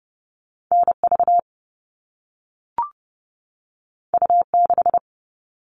Morse code, heard as a keyed tone, about 30 wpm, 700 Hz. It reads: NV U6
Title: All Cut Numbers - 2-Digit 30wpm